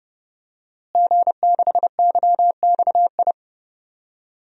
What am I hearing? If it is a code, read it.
Morse code, heard as a keyed tone, about 30 wpm, 700 Hz. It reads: G6YXI
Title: DXCC-100 Call Signs - 5 Characters - Repeated 3x - 30wpm